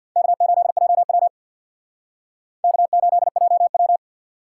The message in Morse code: K8JW K8JW

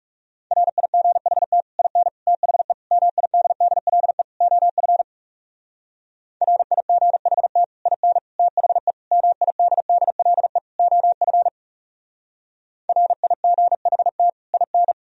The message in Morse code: RIGHT IN THE MIDDLE OF RIGHT IN THE MIDDLE OF RIGHT IN

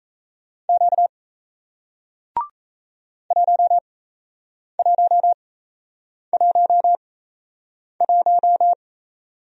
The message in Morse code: Q 1111

